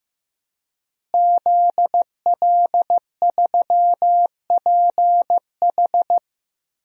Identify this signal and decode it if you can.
Morse code, heard as a keyed tone, about 15 wpm, 700 Hz. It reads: ZL3PH
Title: DXCC-100 Call Signs - 5 Characters - Repeated 2x - 15wpm